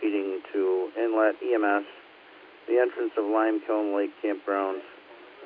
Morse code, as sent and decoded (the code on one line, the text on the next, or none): none